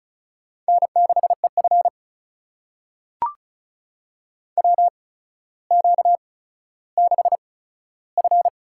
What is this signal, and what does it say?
Morse code, heard as a keyed tone, about 36 wpm, spaced as for 15 wpm, 700 Hz. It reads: N6EF WQ6F
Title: USA Call Signs - 4 Characters - 8x Character Spacing - 35wpm